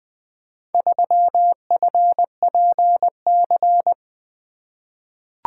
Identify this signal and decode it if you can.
Morse code, heard as a keyed tone, about 20 wpm, 700 Hz. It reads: 3FPC
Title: Four Letter-Number Combinations - 20wpm